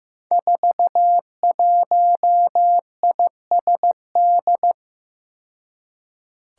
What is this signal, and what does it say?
Morse code, heard as a keyed tone, about 16 wpm, 700 Hz. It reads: 41ISD
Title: Five Challenging Letter-Number Combinations 15wpm